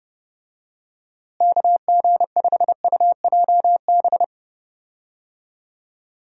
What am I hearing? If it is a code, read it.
Morse code, heard as a keyed tone, about 32 wpm, 700 Hz. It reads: KG5UJB
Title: USA Call Signs - 6 Characters - Repeated 3x - 30wpm